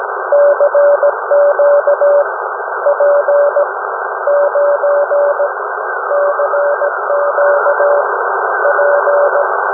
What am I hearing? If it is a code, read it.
Morse code, heard as a keyed tone, about 17 wpm, spaced as for 13 wpm, 580 Hz, with noise. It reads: CQ P9 CQ P
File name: UM01- -0743utc-3581khz